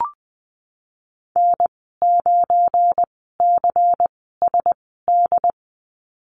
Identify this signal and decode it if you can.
Morse code, heard as a keyed tone, about 19 wpm, spaced as for 14 wpm, 700 Hz. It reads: N9CSD
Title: 173 Lesson 28g - Number 9 - Call Signs 20wpm